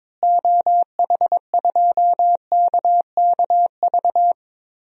Morse code, heard as a keyed tone, about 22 wpm, 700 Hz. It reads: OH2KKV
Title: DXCC-100 Call Signs - 6 Characters - Repeated 3x - 22wpm